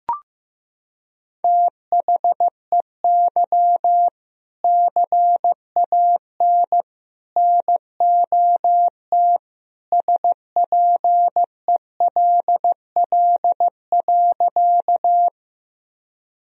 Morse code, heard as a keyed tone, about 16 wpm, 700 Hz. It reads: THEY CAN NOT SPELL.
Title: Sentences from Top 1000 Words - Review Part 1 - 15wpm